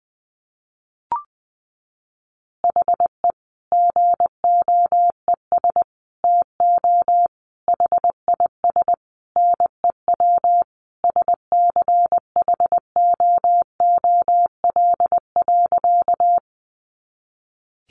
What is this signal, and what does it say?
Morse code, heard as a keyed tone, about 20 wpm, 700 Hz. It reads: HE GOES TO HIS NEW SCHOOL.
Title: Easy Phrases 20wpm